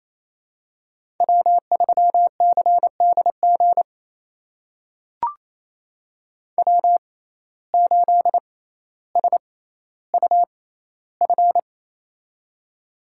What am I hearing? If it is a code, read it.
Morse code, heard as a keyed tone, about 28 wpm, spaced as for 14 wpm, 700 Hz. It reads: W3CDG W8SUF
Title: USA Call Signs - 5 Characters - 6x Character Spacing - 28wpm